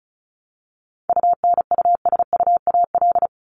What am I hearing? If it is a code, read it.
Morse code, heard as a keyed tone, about 34 wpm, 700 Hz. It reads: UNUSUAL
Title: Top 1251-1500 Words 35wpm